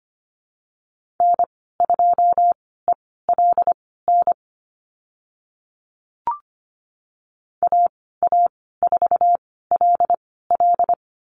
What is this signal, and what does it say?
Morse code, heard as a keyed tone, about 25 wpm, spaced as for 16 wpm, 700 Hz. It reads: N2ELN AA4LL
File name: Lesson 15g - Number 5 - Call Signs 25wpm